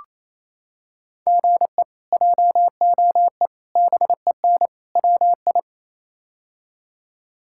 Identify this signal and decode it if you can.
Morse code, heard as a keyed tone, about 27 wpm, 700 Hz. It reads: GE JOE BEN WI
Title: K1USN SST QSO Practice - 28wpm